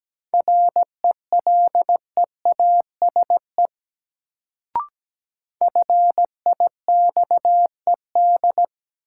Morse code, heard as a keyed tone, about 17 wpm, 700 Hz. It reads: RELEASE FIXED
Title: Sentences from Top 1100 Words - New Word Drill - 17wpm